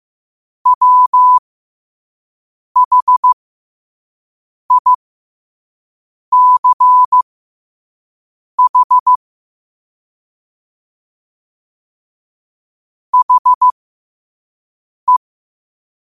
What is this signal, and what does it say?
Morse code, heard as a keyed tone, about 15 wpm, spaced as for 5 wpm, 1000 Hz. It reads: WHICH HE
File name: qotdfiles / 5wpm